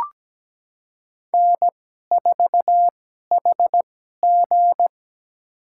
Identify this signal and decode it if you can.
Morse code, heard as a keyed tone, about 17 wpm, spaced as for 12 wpm, 700 Hz. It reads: N4HG
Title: USA Call Signs - 4 Characters - 2x Character Spacing - 17wpm